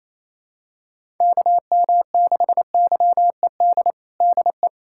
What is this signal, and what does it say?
Morse code, heard as a keyed tone, about 28 wpm, 700 Hz. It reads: KM6YED DE
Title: USA Callsign DE Callsign - 6 Characters - 28wpm